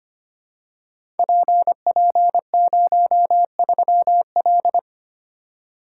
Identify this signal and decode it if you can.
Morse code, heard as a keyed tone, about 25 wpm, 700 Hz. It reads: PP03L
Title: Code Groups - 5 Letter:Numbers - 25wpm